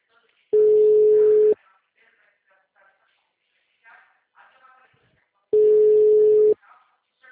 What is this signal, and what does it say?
Ringback tone (Europe, 425 Hz), with speech.
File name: out- -2011- - - .76727